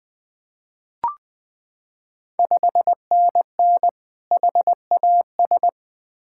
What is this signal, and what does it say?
Morse code, heard as a keyed tone, about 20 wpm, 700 Hz. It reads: 5NN HAS
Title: Ontario QSO Party Exchange 20wpm